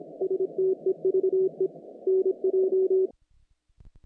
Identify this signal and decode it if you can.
Morse code, heard as a keyed tone, about 25 wpm, 380 Hz. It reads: STEVE NJ
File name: contests / naqp-cw-k5tr-op-k5pi- -1800